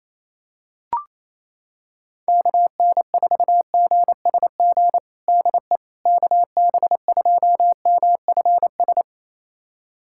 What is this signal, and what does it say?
Morse code, heard as a keyed tone, about 28 wpm, 700 Hz. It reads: KN4GSG DE KB2MFS
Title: USA Callsign DE Callsign - 6 Characters - 28wpm